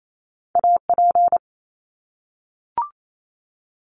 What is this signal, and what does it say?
Morse code, heard as a keyed tone, about 28 wpm, 700 Hz. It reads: AP
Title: Top Two Letter Combinations in Words 28wpm